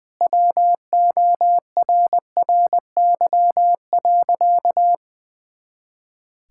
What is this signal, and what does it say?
Morse code, heard as a keyed tone, about 20 wpm, 700 Hz. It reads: WORRY.
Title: Pangram Challenge v2 - 20wpm